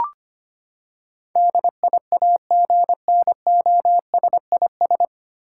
Morse code, heard as a keyed tone, about 25 wpm, 700 Hz. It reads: DIAGNOSIS
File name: Nine Letter Words - 25wpm